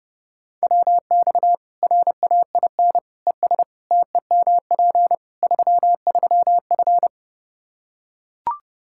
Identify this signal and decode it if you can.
Morse code, heard as a keyed tone, about 30 wpm, 700 Hz. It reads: WX RAIN ES TEMP 33F
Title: Practice QSO - Exchange 2 - Info - 30wpm